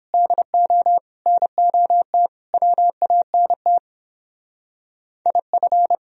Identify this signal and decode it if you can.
Morse code, heard as a keyed tone, about 26 wpm, 700 Hz. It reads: DO NOT WANT IF
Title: Speed-Racing - Sets of 5 Words 22wpm